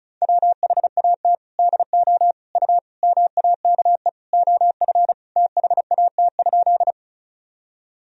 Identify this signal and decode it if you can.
Morse code, heard as a keyed tone, about 36 wpm, 700 Hz. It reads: WHAT DO U MAKE OF THAT?
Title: Sentences from Top 100 Words and Abbreviations 35wpm